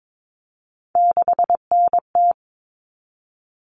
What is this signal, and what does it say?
Morse code, heard as a keyed tone, about 22 wpm, 700 Hz. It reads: 6NT
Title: All Cut Numbers - 3-Digit 22wpm